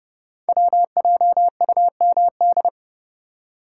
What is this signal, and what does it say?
Morse code, heard as a keyed tone, about 28 wpm, 700 Hz. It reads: WJUMD
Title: Code Groups - 5 Letters - 30wpm